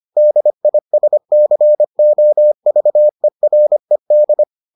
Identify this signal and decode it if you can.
Morse code, heard as a keyed tone, about 25 wpm, 600 Hz. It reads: DISCOVERED